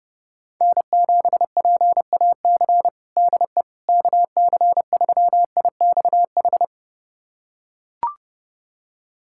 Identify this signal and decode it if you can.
Morse code, heard as a keyed tone, about 30 wpm, 700 Hz. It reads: N7PAC DE KC3IXH
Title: USA Callsign DE Callsign 30wpm